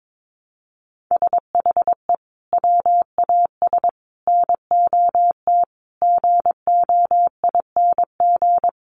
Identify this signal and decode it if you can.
Morse code, heard as a keyed tone, about 22 wpm, 700 Hz. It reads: SHE WAS NOT GOING
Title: Sets of 5 Words Encore2 22wpm